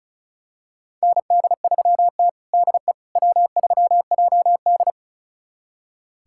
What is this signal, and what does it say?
Morse code, heard as a keyed tone, about 36 wpm, 700 Hz. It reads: ND3T DE W3JD
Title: USA Callsign DE Callsign - 4 Characters - 35wpm